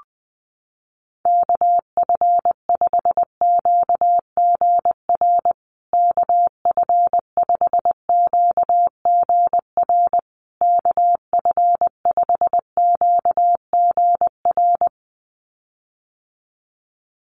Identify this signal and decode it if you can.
Morse code, heard as a keyed tone, about 20 wpm, 700 Hz. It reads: KF5QGR KF5QGR KF5QGR
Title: USA Call Signs Repeated 3x 20wpm